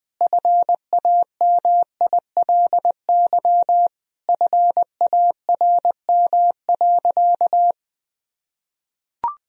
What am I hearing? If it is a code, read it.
Morse code, heard as a keyed tone, about 20 wpm, 700 Hz. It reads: FAMILY FARM.
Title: Sentences from Top 1000 Words - Review Part 2 - 20wpm